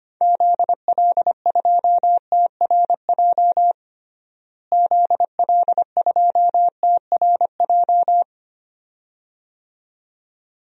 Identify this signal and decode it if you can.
Morse code, heard as a keyed tone, about 25 wpm, 700 Hz. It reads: ZL2TRJ ZL2TRJ
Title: DXCC-100 Call Signs - 6 Characters - Repeated 2x - 25wpm